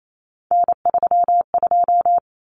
Morse code, heard as a keyed tone, about 28 wpm, 700 Hz. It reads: N32
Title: Common Cut Numbers - 3-Digit 28wpm